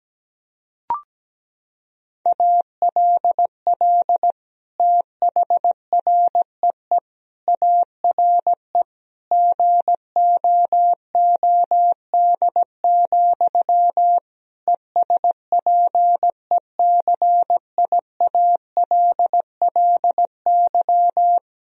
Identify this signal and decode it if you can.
Morse code, heard as a keyed tone, about 17 wpm, 700 Hz. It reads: ALL THREE ARE GOOD, ESPECIALLY
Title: Sentences from Top 1000 Words - Review Part 4 - 17wpm